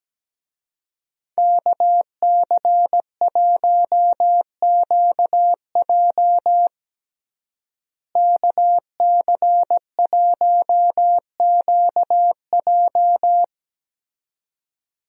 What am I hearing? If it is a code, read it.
Morse code, heard as a keyed tone, about 17 wpm, 700 Hz. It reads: KC1QJ KC1QJ
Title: USA Call Signs - 5 Characters - Repeated 2x - 17wpm